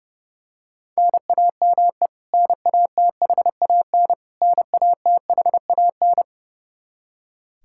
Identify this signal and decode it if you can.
Morse code, heard as a keyed tone, about 30 wpm, 700 Hz. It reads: NAME NATHAN NATHAN
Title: Practice QSO - Exchange 1 - Beginning - 30wpm